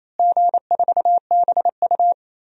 Morse code, heard as a keyed tone, about 28 wpm, 700 Hz. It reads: G4BU
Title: DXCC-100 Call Signs - 4 Characters - Repeated 3x - 28wpm